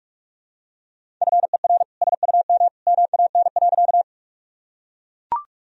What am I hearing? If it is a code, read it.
Morse code, heard as a keyed tone, about 42 wpm, 700 Hz. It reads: FER SUM MAN.
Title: Sentences from Top 200 Words and Abbreviations 45wpm